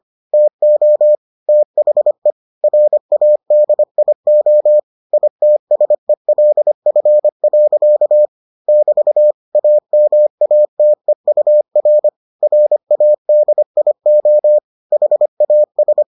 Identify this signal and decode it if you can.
Morse code, heard as a keyed tone, about 25 wpm, 600 Hz. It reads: TO THE RADIO ITSELF. <BT> AMATEUR RADIO HAS